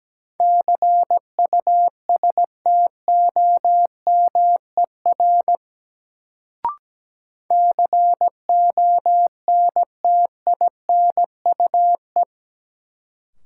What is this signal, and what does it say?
Morse code, heard as a keyed tone, about 17 wpm, 700 Hz. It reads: CUSTOMER CONTINUE
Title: Sentences from Top 1100 Words - New Word Drill Alt - 17wpm